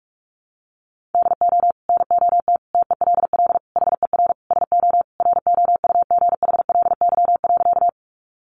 Morse code, heard as a keyed tone, about 44 wpm, 700 Hz. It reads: DO NOT TELL HER SO ROUGHLY.
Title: Sentences from Top 1500 Words 45wpm